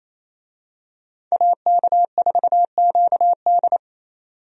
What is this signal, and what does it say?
Morse code, heard as a keyed tone, about 28 wpm, 700 Hz. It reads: AK4QD
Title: USA Call Signs - 5 Characters - 28wpm